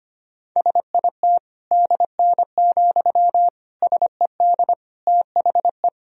Morse code, heard as a keyed tone, about 25 wpm, 700 Hz. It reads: SIT DN, SED THE
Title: Sentences from Top 500 Words and Abbreviations 25wpm